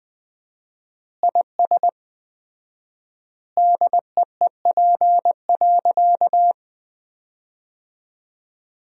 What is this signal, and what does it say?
Morse code, heard as a keyed tone, about 20 wpm, 700 Hz. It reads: IS DEEP.
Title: 4x Word-Spacing - Sentences from Top 500 Words - 20wpm